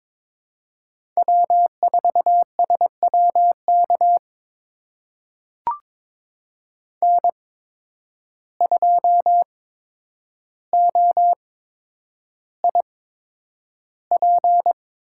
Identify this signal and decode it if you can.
Morse code, heard as a keyed tone, about 22 wpm, spaced as for 9 wpm, 700 Hz. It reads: W4SWK N2OIP
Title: USA Call Signs - 5 Characters - 8x Character Spacing - 22wpm